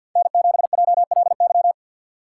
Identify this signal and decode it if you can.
Morse code, heard as a keyed tone, about 49 wpm, 700 Hz. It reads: N7JLY
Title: USA Call Signs - 3x Character Spacing - 50wpm